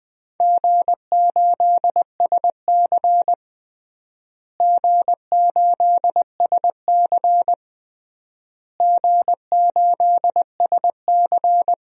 Morse code, heard as a keyed tone, about 21 wpm, 700 Hz. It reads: G8SC G8SC G8SC
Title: DXCC-100 Call Signs - 4 Characters - Repeated 3x - 20wpm